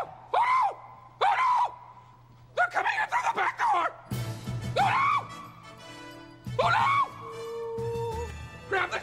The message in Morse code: none